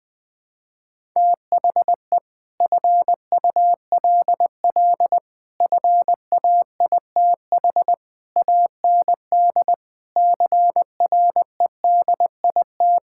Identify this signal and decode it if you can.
Morse code, heard as a keyed tone, about 20 wpm, 700 Hz. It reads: THE FULL FAITH AND CREDIT
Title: Sentences with 7-Word Phrases - Spoken Beginning - 20wpm